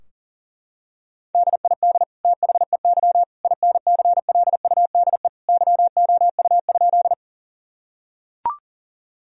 Morse code, heard as a keyed tone, about 38 wpm, 700 Hz. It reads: DID THEY INCLUDE YOU?
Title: Sentences from Top 1000 Words - Review Part 3 - 40wpm